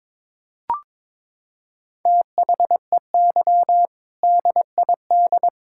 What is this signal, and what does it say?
Morse code, heard as a keyed tone, about 22 wpm, 700 Hz. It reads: THEY DID
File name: practice / 22wpm / Sentences with 3-Word Phrases - 22wpm